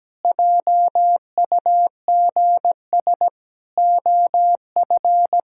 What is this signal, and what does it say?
Morse code, heard as a keyed tone, about 17 wpm, 700 Hz. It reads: JUGS OF
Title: Pangram Challenge v2 - 17wpm